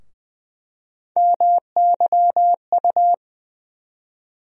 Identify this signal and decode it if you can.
Morse code, heard as a keyed tone, about 20 wpm, 700 Hz. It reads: MYU
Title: USA Call Sign Postfixes - 3 Letters 20wpm